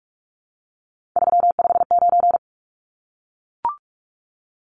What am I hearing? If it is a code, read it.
Morse code, heard as a keyed tone, about 44 wpm, 700 Hz. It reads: 359
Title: Numbers - 3-Digit 45wpm